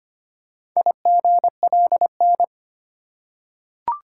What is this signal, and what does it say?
Morse code, heard as a keyed tone, about 25 wpm, 700 Hz. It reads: IGLN
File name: practice / 25wpm / Four Letter-Number Combinations - 25wpm